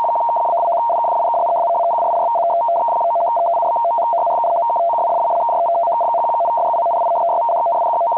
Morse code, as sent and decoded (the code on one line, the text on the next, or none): none